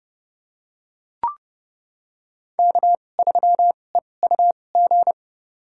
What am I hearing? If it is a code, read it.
Morse code, heard as a keyed tone, about 32 wpm, spaced as for 22 wpm, 700 Hz. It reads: K3EUG
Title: USA Call Signs - 5 Characters - 2x Character Spacing - 30wpm